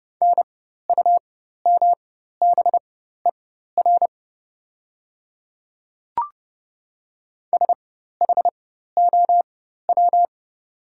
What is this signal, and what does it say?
Morse code, heard as a keyed tone, about 30 wpm, spaced as for 14 wpm, 700 Hz. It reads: NUMBER SHOW
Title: 4x Character Spacing - Top 200 Words - 30wpm